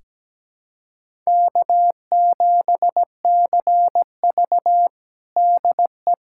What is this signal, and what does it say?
Morse code, heard as a keyed tone, about 17 wpm, 700 Hz. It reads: K7CV DE
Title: USA Callsign DE Callsign - 4 Characters - 17wpm